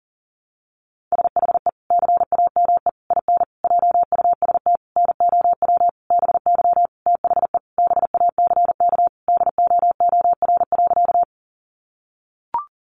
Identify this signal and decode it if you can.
Morse code, heard as a keyed tone, about 40 wpm, 700 Hz. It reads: SHE CAME IN JUST NOW BY THE BACK DOOR.
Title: Sentences from Top 300 Words 40wpm